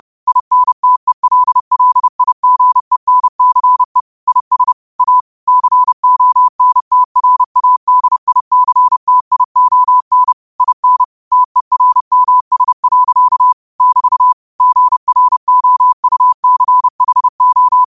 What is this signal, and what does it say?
Morse code, heard as a keyed tone, about 30 wpm, 1000 Hz. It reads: INTELLIGENCE IS A CONTRADICTION IN TERMS. <BT> GROUCHO
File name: qotdfiles / 30wpm